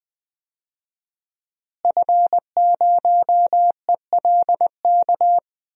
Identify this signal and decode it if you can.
Morse code, heard as a keyed tone, about 20 wpm, 700 Hz. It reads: F0ELK
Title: DXCC-100 Call Signs - 5 Characters - Repeated 2x - 20wpm